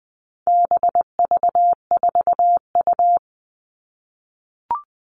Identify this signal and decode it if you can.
Morse code, heard as a keyed tone, about 20 wpm, 700 Hz. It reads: BV4U
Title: All Cut Numbers - 4-Digit 20wpm